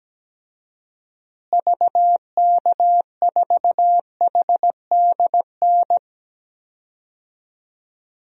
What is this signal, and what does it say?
Morse code, heard as a keyed tone, about 17 wpm, 700 Hz. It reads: VK4HDN
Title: DXCC-100 Call Signs - 6 Characters - Repeated 3x - 17wpm